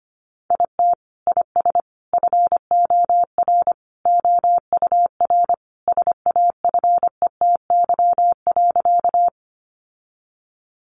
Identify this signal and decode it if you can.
Morse code, heard as a keyed tone, about 25 wpm, 700 Hz. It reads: IT IS FOR OUR SAFETY.
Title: Sentences from Top 700 Words 25wpm